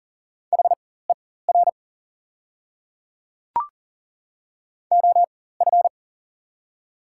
Morse code, heard as a keyed tone, about 40 wpm, spaced as for 19 wpm, 700 Hz. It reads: HER OF
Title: 4x Character Spacing - Top 300 Words - 40wpm